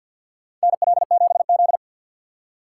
Morse code, heard as a keyed tone, about 53 wpm, 700 Hz. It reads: NL7Z
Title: USA Call Signs - 6x Character Spacing - 50wpm